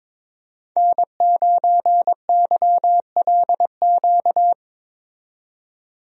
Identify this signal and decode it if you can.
Morse code, heard as a keyed tone, about 22 wpm, 700 Hz. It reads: N9YLQ
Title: USA Call Signs - 5 Characters - 22wpm